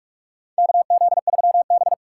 Morse code, heard as a keyed tone, about 43 wpm, 700 Hz. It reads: KZ3B
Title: USA Call Signs - 4 Characters - 6x Character Spacing - 45wpm